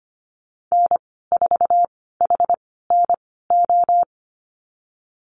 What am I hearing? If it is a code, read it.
Morse code, heard as a keyed tone, about 25 wpm, spaced as for 16 wpm, 700 Hz. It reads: N4HNO
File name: Lesson 15g - Number 5 - Call Signs 25wpm